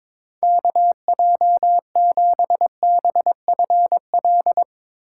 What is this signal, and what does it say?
Morse code, heard as a keyed tone, about 22 wpm, 700 Hz. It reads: KJ7BFL
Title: USA Call Signs - 6 Characters - 4x Character Spacing - 22wpm